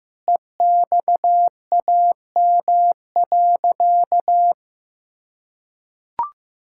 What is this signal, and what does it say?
Morse code, heard as a keyed tone, about 15 wpm, 700 Hz. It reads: EXAM.
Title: Sentences from Top 1000 Words - Review Part 5 - 15wpm